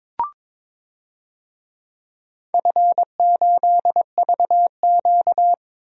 Morse code, heard as a keyed tone, about 21 wpm, 700 Hz. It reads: F8VQ
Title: DXCC-100 Call Signs - 4 Characters - 22wpm